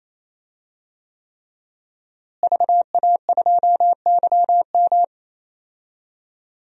Morse code, heard as a keyed tone, about 28 wpm, 700 Hz. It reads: VA2YM
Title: DXCC-100 Call Signs - 5 Characters - Repeated 3x - 28wpm